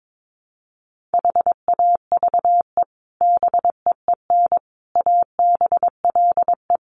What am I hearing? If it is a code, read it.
Morse code, heard as a keyed tone, about 22 wpm, 700 Hz. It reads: HAVE BEEN ABLE
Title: Sets of 4 Words 22wpm